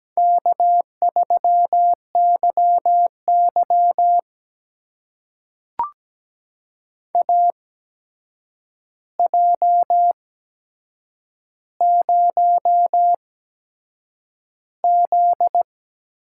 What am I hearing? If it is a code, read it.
Morse code, heard as a keyed tone, about 17 wpm, spaced as for 7 wpm, 700 Hz. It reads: K3YY AJ0Z
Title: USA Call Signs - 8x Character Spacing - 17wpm